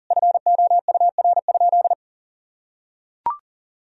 Morse code, heard as a keyed tone, about 40 wpm, 700 Hz. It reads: FOUR?
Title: Sentences from Top 1000 Words - Review Part 2 - 40wpm